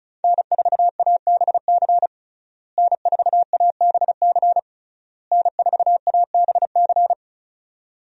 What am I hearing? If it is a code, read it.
Morse code, heard as a keyed tone, about 36 wpm, 700 Hz. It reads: N4ABC N4ABC N4ABC
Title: USA Call Signs - 5 Characters - Repeated 3x - 35wpm